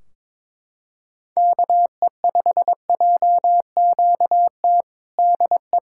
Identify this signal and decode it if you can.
Morse code, heard as a keyed tone, about 22 wpm, 700 Hz. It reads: KE5JQT DE
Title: Practice QSO - Exchange 4 - Ending - 22wpm